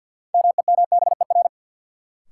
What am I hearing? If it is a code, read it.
Morse code, heard as a keyed tone, about 52 wpm, 700 Hz. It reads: MEMBER
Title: Sentences from Top 1100 Words - New Word Drill Alt - 50wpm